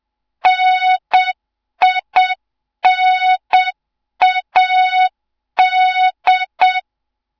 Morse code, heard as a keyed tone, about 7 wpm, 3700 Hz. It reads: NINAD
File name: Morze-HA2MN / hang / 07 GYAKORLAT